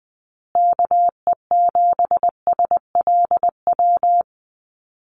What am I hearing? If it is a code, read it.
Morse code, heard as a keyed tone, about 20 wpm, 700 Hz. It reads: KE7SLW
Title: USA Call Signs Repeated 2x 20wpm